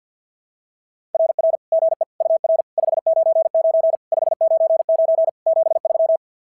Morse code, heard as a keyed tone, about 44 wpm, 660 Hz. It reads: RR GE UR 599 599 73
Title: SOTA QSO Practice 50wpm